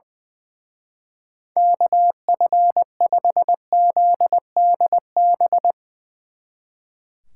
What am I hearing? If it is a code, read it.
Morse code, heard as a keyed tone, about 23 wpm, 700 Hz. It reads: KF5ZDB
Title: Speed Racing - USA 2x3 Call Signs 20wpm